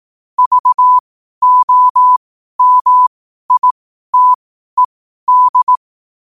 Morse code, heard as a keyed tone, about 18 wpm, spaced as for 12 wpm, 1000 Hz. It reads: VOMITED
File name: qotdfiles / 12wpm